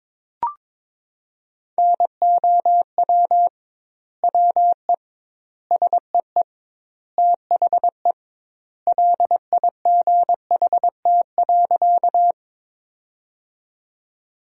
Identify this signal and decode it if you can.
Morse code, heard as a keyed tone, about 22 wpm, 700 Hz. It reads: NOW WE SEE THE LIGHT.
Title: 2x Word-Spacing - Sentences from Top 200 Words - 22wpm